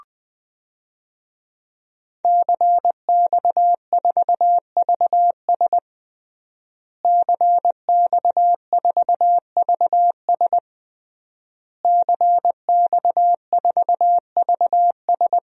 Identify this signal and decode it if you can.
Morse code, heard as a keyed tone, about 20 wpm, 700 Hz. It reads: CX4VS CX4VS CX4VS
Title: DXCC-100 Call Signs - 5 Characters - Repeated 3x - 20wpm